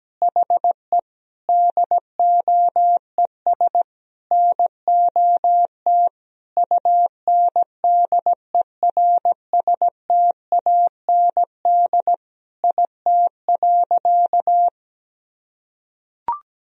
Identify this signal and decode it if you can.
Morse code, heard as a keyed tone, about 17 wpm, 700 Hz. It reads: HE DOES NOT UNDERSTAND IT.
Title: Sentences from Top 1000 Words - Review Part 2 - 17wpm